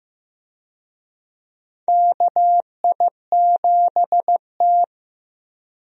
Morse code, heard as a keyed tone, about 15 wpm, 700 Hz. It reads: KI7T
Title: USA Call Signs - 4 Characters - Repeated 3x - 15wpm